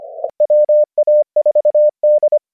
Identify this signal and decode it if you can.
Morse code, heard as a keyed tone, about 25 wpm, 600 Hz. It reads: WA4D